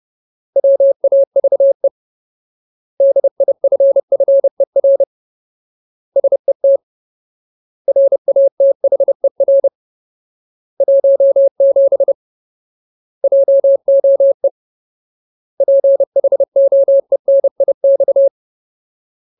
Morse code, heard as a keyed tone, about 30 wpm, 550 Hz. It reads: WAVE DIFFER SET RATHER 17 JOE PHOENIX
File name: Ultimate Word List / 30WPM Ultimate